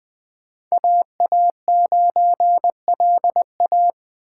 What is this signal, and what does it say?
Morse code, heard as a keyed tone, about 24 wpm, 700 Hz. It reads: AA9LA
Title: Speed Racing - USA 2x2 Call Signs 20wpm